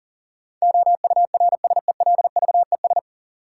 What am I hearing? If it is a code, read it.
Morse code, heard as a keyed tone, about 40 wpm, 700 Hz. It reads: OURSELVES
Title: Sentences from Top 1600 Words - New Word Drill Alt - 40wpm